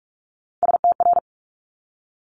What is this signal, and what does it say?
Morse code, heard as a keyed tone, about 46 wpm, 700 Hz. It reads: STR
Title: Multi-letter Phonemes 45wpm